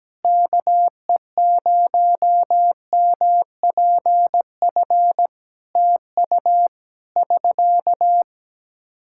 Morse code, heard as a keyed tone, about 17 wpm, 700 Hz. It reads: KE0MPF TU <SK>
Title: Practice QSO - Exchange 4 - Ending - 17wpm